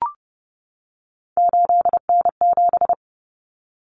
Morse code, heard as a keyed tone, about 30 wpm, 700 Hz. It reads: 8N7
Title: Common Cut Numbers - 3-Digit 30wpm